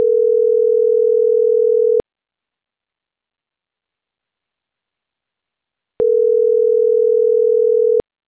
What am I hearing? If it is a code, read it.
Ringback tone (North America), 440 + 480 Hz.